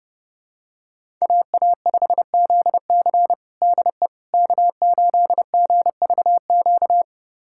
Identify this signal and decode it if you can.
Morse code, heard as a keyed tone, about 32 wpm, 700 Hz. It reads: AA5ZC DE K8GVQ
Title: USA Callsign DE Callsign - 5 Characters - 30wpm